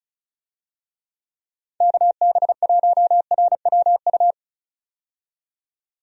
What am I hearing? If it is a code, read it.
Morse code, heard as a keyed tone, about 34 wpm, 700 Hz. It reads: KB1RWU
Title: USA Call Signs - 6 Characters - 35wpm